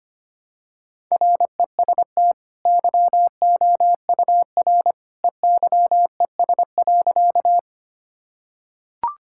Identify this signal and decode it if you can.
Morse code, heard as a keyed tone, about 25 wpm, 700 Hz. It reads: REST YOUR EYES.
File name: Easy Sentences with Hints - 25wpm